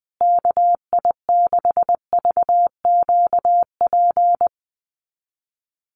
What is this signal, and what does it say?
Morse code, heard as a keyed tone, about 20 wpm, 700 Hz. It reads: KI6VQP
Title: USA Call Signs Repeated 3x 20wpm